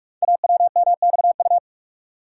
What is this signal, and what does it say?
Morse code, heard as a keyed tone, about 44 wpm, 700 Hz. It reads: AWMXU
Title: Code Groups - 5 Letters - 45wpm